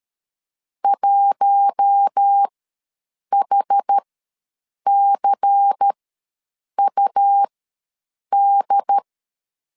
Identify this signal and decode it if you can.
Morse code, heard as a keyed tone, about 13 wpm, spaced as for 7 wpm, 800 Hz. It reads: 1HCUD
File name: LEZIONI / 5 LBVC / 5 LBVC ws 12wpm 3159